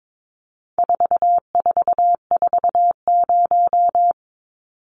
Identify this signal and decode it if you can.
Morse code, heard as a keyed tone, about 22 wpm, 700 Hz. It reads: 4440
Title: Numbers - 4-Digit 22wpm